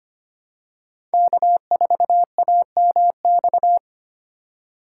Morse code, heard as a keyed tone, about 25 wpm, 700 Hz. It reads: K4AMX
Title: USA Call Signs - 5 Characters - 4x Character Spacing - 25wpm